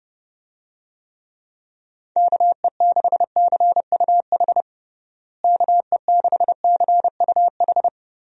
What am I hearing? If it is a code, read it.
Morse code, heard as a keyed tone, about 32 wpm, 700 Hz. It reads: KE6CUH KE6CUH
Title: USA Call Signs - 6 Characters - Repeated 3x - 30wpm